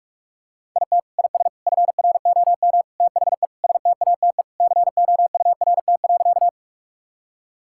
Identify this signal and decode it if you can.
Morse code, heard as a keyed tone, about 45 wpm, 700 Hz. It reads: IT IS FROM THE STATE COURT.
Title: Sentences from Top 1600 Words 45wpm